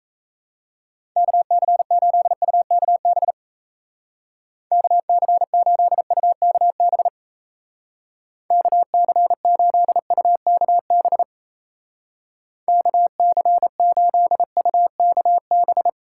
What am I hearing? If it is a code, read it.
Morse code, heard as a keyed tone, about 34 wpm, 700 Hz. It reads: KC8UKB KC8UKB KC8UKB KC8UKB
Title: Speed Racing - USA 2x3 Call Signs 28wpm